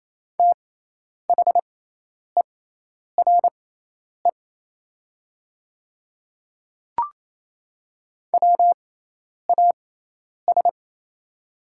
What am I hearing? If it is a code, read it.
Morse code, heard as a keyed tone, about 28 wpm, spaced as for 10 wpm, 700 Hz. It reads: THERE WAS
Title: 6x Character Spacing - Top 500 Words - 28wpm